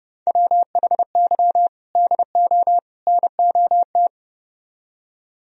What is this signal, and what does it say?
Morse code, heard as a keyed tone, about 30 wpm, 700 Hz. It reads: WHY DO NOT